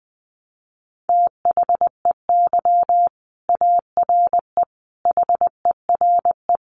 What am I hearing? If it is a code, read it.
Morse code, heard as a keyed tone, about 20 wpm, 700 Hz. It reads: THEY ARE HERE
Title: Sentences from Top 1100 Words 20wpm